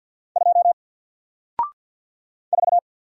Morse code, heard as a keyed tone, about 48 wpm, 700 Hz. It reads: 2 4
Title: Single Character (Letter:Number:Punctuation:Prosign) Alt 50wpm